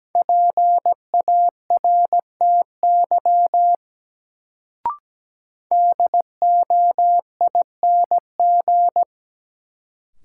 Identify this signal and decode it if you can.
Morse code, heard as a keyed tone, about 17 wpm, 700 Hz. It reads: PARTY DOING
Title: Sentences from Top 1400 Words - New Word Drill Alt - 17wpm